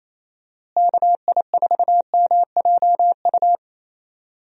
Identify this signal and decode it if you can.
Morse code, heard as a keyed tone, about 28 wpm, 700 Hz. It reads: KI4MJU
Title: USA Call Signs - 6 Characters - Repeated 3x - 28wpm